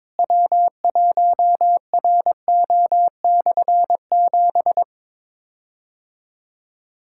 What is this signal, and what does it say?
Morse code, heard as a keyed tone, about 22 wpm, 700 Hz. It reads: W1RO/7
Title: K1USN SST QSO Practice - 22wpm